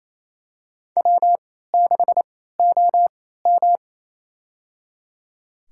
Morse code, heard as a keyed tone, about 28 wpm, spaced as for 16 wpm, 700 Hz. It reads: W6OM
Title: USA Call Signs - 4 Characters - 3x Character Spacing - 28wpm